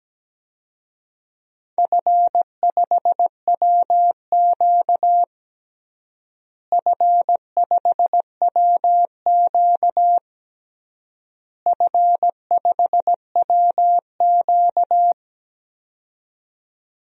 Morse code, heard as a keyed tone, about 17 wpm, 700 Hz. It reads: F5WQ F5WQ F5WQ
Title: DXCC-100 Call Signs - 4 Characters - Repeated 3x - 17wpm